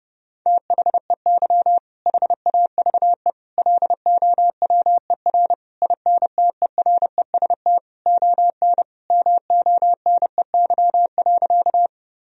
Morse code, heard as a keyed tone, about 30 wpm, 700 Hz. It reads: THEY HAVE LOWER INTEREST ON MONEY.
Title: Sentences from Top 1000 Words - Review Part 4 - 30wpm